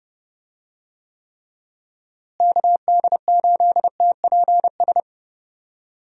Morse code, heard as a keyed tone, about 32 wpm, 700 Hz. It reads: KD8TPS
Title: USA Call Signs - 6 Characters - Repeated 2x - 30wpm